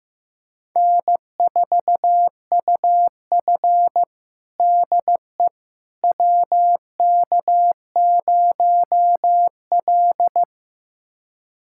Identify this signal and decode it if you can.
Morse code, heard as a keyed tone, about 15 wpm, 700 Hz. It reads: N4UF DE WK0L
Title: USA Callsign DE Callsign - 4 Characters - 15wpm